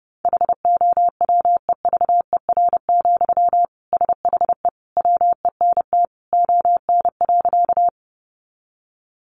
Morse code, heard as a keyed tone, about 30 wpm, 700 Hz. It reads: HOWEVER, SHE WENT ON.
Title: Sentences from Top 600 Words 30wpm